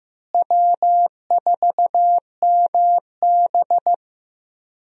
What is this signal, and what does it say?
Morse code, heard as a keyed tone, about 15 wpm, 700 Hz. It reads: W4MB
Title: USA Call Signs - 4 Characters - Repeated 2x - 15wpm